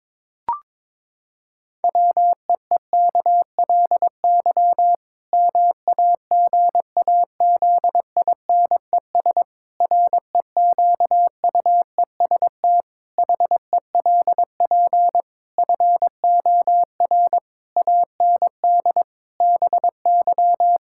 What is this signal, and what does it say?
Morse code, heard as a keyed tone, about 22 wpm, 700 Hz. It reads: WEEKLY MAGAZINES REQUEST HELP FOR AND BY
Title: Pangram Challenge v2 - 22wpm